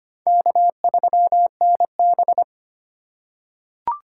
Morse code, heard as a keyed tone, about 25 wpm, 700 Hz. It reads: K3NB
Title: USA Call Signs - 4 Characters - 4x Character Spacing - 25wpm